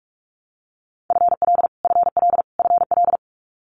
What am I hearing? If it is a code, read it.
Morse code, heard as a keyed tone, about 46 wpm, 700 Hz. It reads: FL FL FL
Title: Multi-letter Phonemes Repeated 3x 45wpm